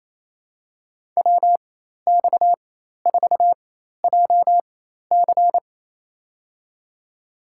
Morse code, heard as a keyed tone, about 28 wpm, spaced as for 13 wpm, 700 Hz. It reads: WX4JC
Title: USA Call Signs - 5 Characters - 4x Character Spacing - 28wpm